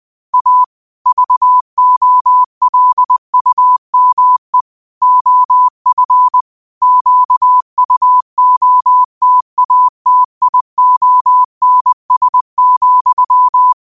Morse code, heard as a keyed tone, about 20 wpm, 1000 Hz. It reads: A VOLUME OF QUOTATIONS,
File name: qotdfiles / 20wpm